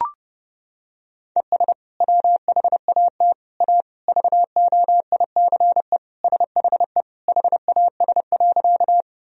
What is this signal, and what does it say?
Morse code, heard as a keyed tone, about 30 wpm, 700 Hz. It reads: ES WHAT A VOICE SHE HAS.
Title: Sentences from Top 500 Words and Abbreviations 30wpm